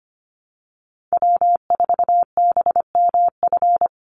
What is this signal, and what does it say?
Morse code, heard as a keyed tone, about 25 wpm, 700 Hz. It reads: W4BMF
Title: USA Call Signs 25wpm